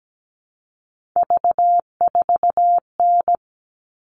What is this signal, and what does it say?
Morse code, heard as a keyed tone, about 17 wpm, 700 Hz. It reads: V4N
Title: All Cut Numbers - 3-Digit 17wpm